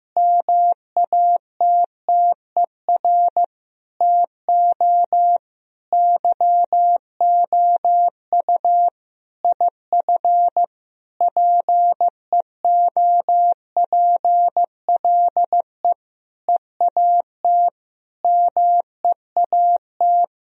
Morse code, heard as a keyed tone, about 15 wpm, 700 Hz. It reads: MATTER TO YOU IF PEOPLE EAT MEAT
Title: Sentences from Top 1000 Words - Review Part 3 - 15wpm